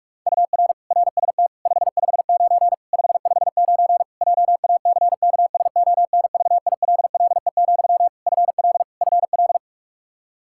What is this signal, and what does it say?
Morse code, heard as a keyed tone, about 46 wpm, 700 Hz. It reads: UR RST 559 559 JACKSONVILLE, FL FL
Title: Practice QSO - Exchange 1 - Beginning - 45wpm